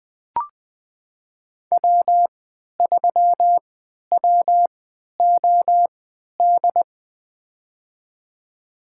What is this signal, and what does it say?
Morse code, heard as a keyed tone, about 20 wpm, spaced as for 11 wpm, 700 Hz. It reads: W3WOD
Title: USA Call Signs - 5 Characters - 3x Character Spacing - 20wpm